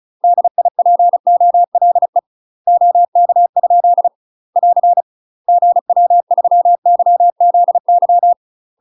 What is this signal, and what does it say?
Morse code, heard as a keyed tone, about 36 wpm, 700 Hz. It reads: DIPOLE OK? <AR> GW3YZY
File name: UK MASTER QSOs TRAINING / 35WPM / UK MASTER QSOs 0041